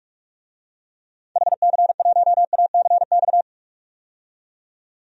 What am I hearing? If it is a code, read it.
Morse code, heard as a keyed tone, about 41 wpm, 700 Hz. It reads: HC1ACX
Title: DXCC-100 Call Signs - 6 Characters - 45wpm